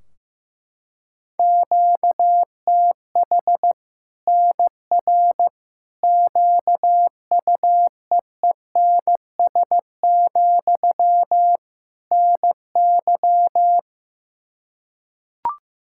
Morse code, heard as a keyed tone, about 15 wpm, 700 Hz. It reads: QTH NR QUEENS, NY
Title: Practice QSO - Exchange 1 - Beginning - 15wpm